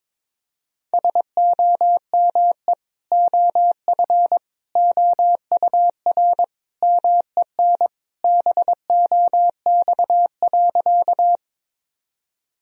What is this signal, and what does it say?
Morse code, heard as a keyed tone, about 21 wpm, 700 Hz. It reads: SOME OF OUR MEN BOX.
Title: Sentences from Top 1000 Words - Review Part 2 - 22wpm